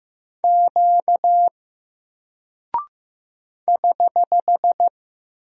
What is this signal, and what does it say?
Morse code, heard as a keyed tone, about 15 wpm, 700 Hz. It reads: Q <HH>
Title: Single Character (Letter:Number:Punctuation:Prosign) Alt 15wpm